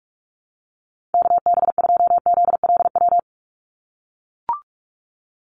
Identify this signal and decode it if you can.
Morse code, heard as a keyed tone, about 44 wpm, 700 Hz. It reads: KB2ZLW
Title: USA Call Signs 45wpm